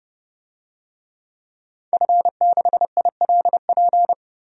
Morse code, heard as a keyed tone, about 32 wpm, 700 Hz. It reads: F6ILP
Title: DXCC-100 Call Signs - 5 Characters - 30wpm